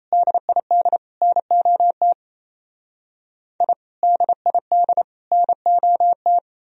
Morse code, heard as a keyed tone, about 35 wpm, 700 Hz. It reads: DID NOT I DID NOT